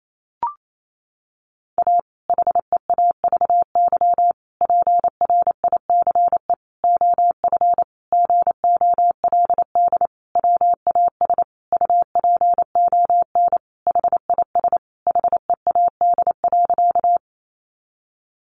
Morse code, heard as a keyed tone, about 28 wpm, 700 Hz. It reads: A HEAVY PRICE OF GOLD WAS UPON HIS HEAD.
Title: Sentences from Top 1100 Words 28wpm